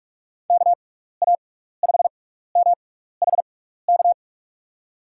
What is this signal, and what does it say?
Morse code, heard as a keyed tone, about 44 wpm, spaced as for 15 wpm, 700 Hz. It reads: KA5MHK
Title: USA Call Signs - 6 Characters - 6x Character Spacing - 45wpm